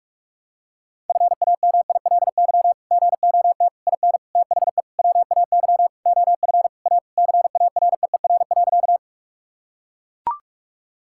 Morse code, heard as a keyed tone, about 41 wpm, 700 Hz. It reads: FAMILY GOT IN THE WAY OF A CAREER.
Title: Sentences from Top 1000 Words - Review Part 6 - 45wpm